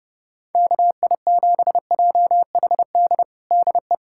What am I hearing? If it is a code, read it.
Morse code, heard as a keyed tone, about 32 wpm, 700 Hz. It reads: KI7JHD DE
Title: USA Callsign DE Callsign 30wpm